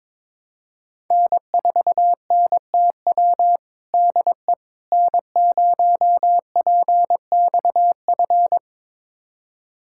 Morse code, heard as a keyed tone, about 22 wpm, 700 Hz. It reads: N4NTW DE N0PXF
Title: USA Callsign DE Callsign - 5 Characters - 22wpm